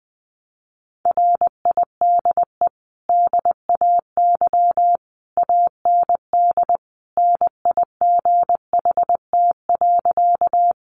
Morse code, heard as a keyed tone, about 20 wpm, 700 Hz. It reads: RIDE DAY AND NIGHT.
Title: Sentences from Top 1200 Words 20wpm